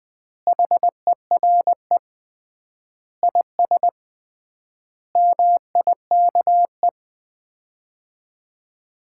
Morse code, heard as a keyed tone, about 20 wpm, 700 Hz. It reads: HERE IS MIKE
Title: Common QSO Phrases - 3x Word-Spacing - 20wpm